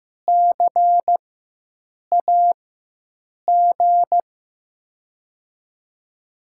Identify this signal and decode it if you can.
Morse code, heard as a keyed tone, about 16 wpm, spaced as for 7 wpm, 700 Hz. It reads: CAG
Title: USA Call Signs - 4x Character Spacing - 15wpm